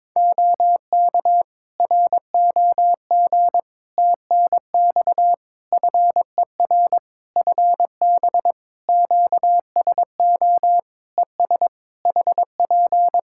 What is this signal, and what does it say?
Morse code, heard as a keyed tone, about 22 wpm, 700 Hz. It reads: OK ROG TNX FER FB QSO ES HP
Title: Practice QSO - Exchange 4 - Ending - 22wpm